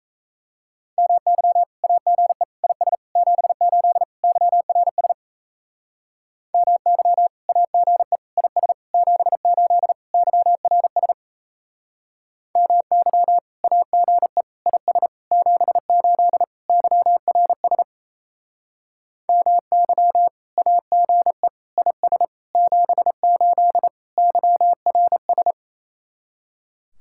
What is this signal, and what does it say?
Morse code, heard as a keyed tone, about 36 wpm, 700 Hz. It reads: MY AGE IS 78 YRS MY AGE IS 78 YRS MY AGE IS 78 YRS MY AGE IS 78 YRS